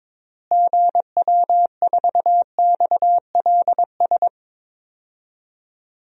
Morse code, heard as a keyed tone, about 22 wpm, 700 Hz. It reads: GW4XLS
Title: DXCC-100 Call Signs - 6 Characters - Repeated 3x - 22wpm